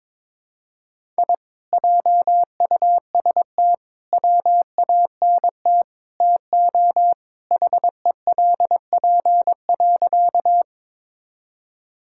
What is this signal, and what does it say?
Morse code, heard as a keyed tone, about 22 wpm, 700 Hz. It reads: I JUST WANT TO HELP.
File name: practice / 22wpm / Sentences with 4-Word Phrases - 22wpm